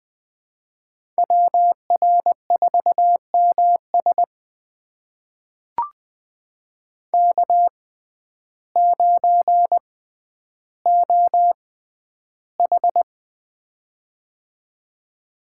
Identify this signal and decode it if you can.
Morse code, heard as a keyed tone, about 21 wpm, spaced as for 11 wpm, 700 Hz. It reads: WR4MS K9OH
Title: USA Call Signs - 6x Character Spacing - 20wpm